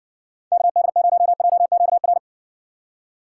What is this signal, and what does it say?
Morse code, heard as a keyed tone, about 61 wpm, 700 Hz. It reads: KD0JQR